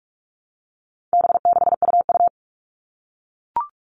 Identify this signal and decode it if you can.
Morse code, heard as a keyed tone, about 44 wpm, 700 Hz. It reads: B6UU
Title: All Cut Numbers - 4-Digit 45wpm